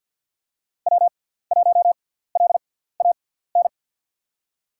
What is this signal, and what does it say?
Morse code, heard as a keyed tone, about 48 wpm, spaced as for 17 wpm, 700 Hz. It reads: W1LAN
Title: USA Call Signs - 5 Characters - 6x Character Spacing - 50wpm